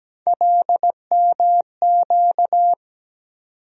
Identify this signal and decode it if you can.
Morse code, heard as a keyed tone, about 17 wpm, 700 Hz. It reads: LMQ
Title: USA Call Sign Postfixes - 3 Letters 17wpm